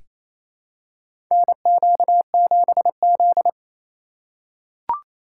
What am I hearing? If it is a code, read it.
Morse code, heard as a keyed tone, about 28 wpm, 700 Hz. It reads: NQ7Z
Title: USA Call Signs - 4 Characters - 4x Character Spacing - 28wpm